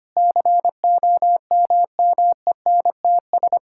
Morse code, heard as a keyed tone, about 25 wpm, 700 Hz. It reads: COMMENTS
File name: practice / 25wpm / Sentences with 9-Word Phrases - 25wpm